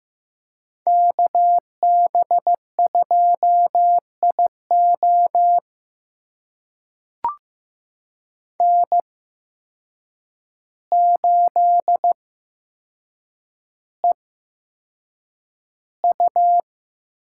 Morse code, heard as a keyed tone, about 15 wpm, spaced as for 7 wpm, 700 Hz. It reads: KB2IO N8EU
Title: USA Call Signs - 5 Characters - 8x Character Spacing - 15wpm